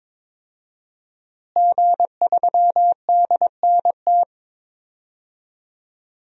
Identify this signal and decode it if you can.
Morse code, heard as a keyed tone, about 22 wpm, 700 Hz. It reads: G3DNT
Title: DXCC-100 Call Signs - 5 Characters - Repeated 3x - 22wpm